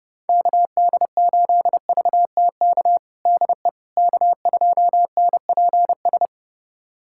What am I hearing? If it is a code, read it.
Morse code, heard as a keyed tone, about 30 wpm, 700 Hz. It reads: KD8VTK DE K2NPS
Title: USA Callsign DE Callsign 30wpm